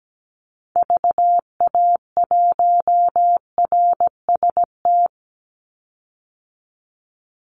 Morse code, heard as a keyed tone, about 17 wpm, 700 Hz. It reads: VA1RST
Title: Call Signs Repeated 3x - 17wpm